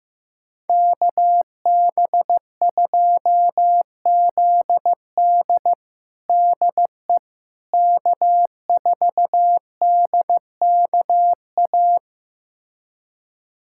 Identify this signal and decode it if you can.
Morse code, heard as a keyed tone, about 15 wpm, 700 Hz. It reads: KB2ZD DE K4DKA
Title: USA Callsign DE Callsign - 5 Characters - 15wpm